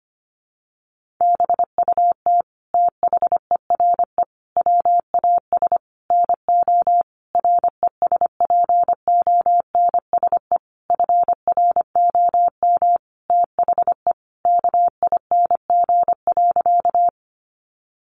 Morse code, heard as a keyed tone, about 25 wpm, 700 Hz. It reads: BUT THERE WAS NO RESPONSE FROM THE KING.
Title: Sentences from Top 1400 Words 25wpm